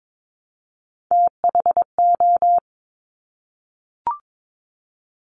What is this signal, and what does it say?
Morse code, heard as a keyed tone, about 22 wpm, 700 Hz. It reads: THO
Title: Top Three Letter Combinations in Words 22wpm